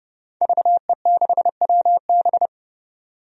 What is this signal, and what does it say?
Morse code, heard as a keyed tone, about 30 wpm, 700 Hz. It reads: VE6WB
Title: DXCC-100 Call Signs - 5 Characters - 30wpm